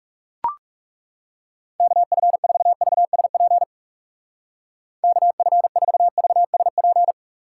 Morse code, heard as a keyed tone, about 39 wpm, 700 Hz. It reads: KF4VSP KF4VSP